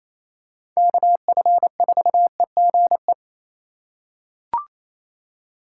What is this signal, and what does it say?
Morse code, heard as a keyed tone, about 28 wpm, 700 Hz. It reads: KF4EGE
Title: USA Call Signs - 6 Characters - 2x Character Spacing - 28wpm